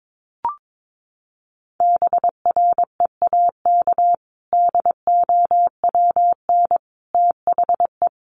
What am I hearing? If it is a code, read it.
Morse code, heard as a keyed tone, about 22 wpm, 700 Hz. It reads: BREAK DOWN THE
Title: Sentences from Top 900 Words 22wpm